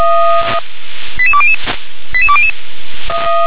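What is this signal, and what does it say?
Five-tone selective call (ZVEI), heard as quick runs of five tones, tones of about 70 ms. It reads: 88200 88200
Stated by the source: ZVEI five-tone call received by SDR.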